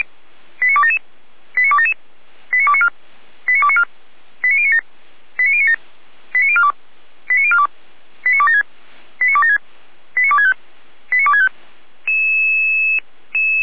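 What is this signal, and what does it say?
Five-tone selective call (ZVEI), heard as quick runs of five tones, tones of about 70 ms. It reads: 89277 89277 89294 89294 89097 89097 89042 89042 89276 89276 89265 89265
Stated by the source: ZVEI five-tone call received by SDR.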